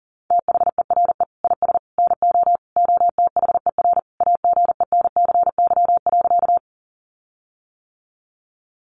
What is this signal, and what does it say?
Morse code, heard as a keyed tone, about 40 wpm, 700 Hz. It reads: THERE IS NO OTHER AGENCY.
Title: Sentences from Top 1300 Words 40wpm